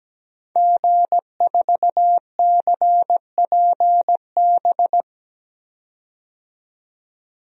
Morse code, heard as a keyed tone, about 17 wpm, 700 Hz. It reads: G4CPB
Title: DXCC-100 Call Signs - 5 Characters - 17wpm